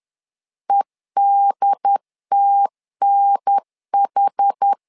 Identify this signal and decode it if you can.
Morse code, heard as a keyed tone, about 10 wpm, 800 Hz. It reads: EDTNH
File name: LEZIONI / 4 Q 10wpm 2210